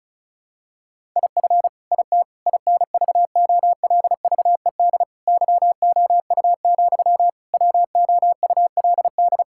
Morse code, heard as a keyed tone, about 36 wpm, 700 Hz. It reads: IF IT INVOLVED YOU, WOULD
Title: Sentences from Top 1000 Words - Review Part 5 - 35wpm